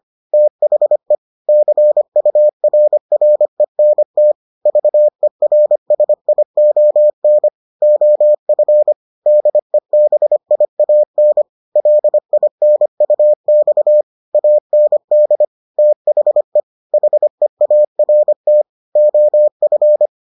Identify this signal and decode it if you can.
Morse code, heard as a keyed tone, about 25 wpm, 600 Hz. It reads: THE CURRENT VERSION OF DEBIAN LINUX AND THE HEART OF